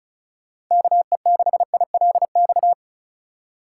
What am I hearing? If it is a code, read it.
Morse code, heard as a keyed tone, about 34 wpm, 700 Hz. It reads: KE6ILX
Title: USA Call Signs - 6 Characters - 4x Character Spacing - 35wpm